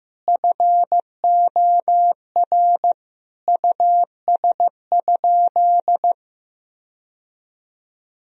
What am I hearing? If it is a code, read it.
Morse code, heard as a keyed tone, about 15 wpm, 700 Hz. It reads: FOR US?
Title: Sentences from Top 1000 Words - Review Part 3 - 15wpm